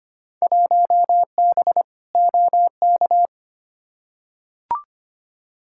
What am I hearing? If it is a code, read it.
Morse code, heard as a keyed tone, about 25 wpm, 700 Hz. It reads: 1B OK
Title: Field Day Exchanges Encore - 25wpm